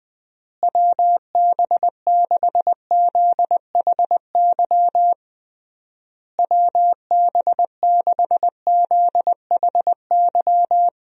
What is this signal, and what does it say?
Morse code, heard as a keyed tone, about 20 wpm, 700 Hz. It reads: WB6ZHY WB6ZHY
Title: USA Call Signs - 6 Characters - Repeated 3x - 20wpm